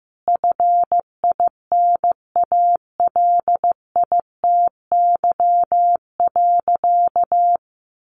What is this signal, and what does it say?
Morse code, heard as a keyed tone, about 15 wpm, 700 Hz. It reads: FINALITY.
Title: A Sound Of Thunder 15wpm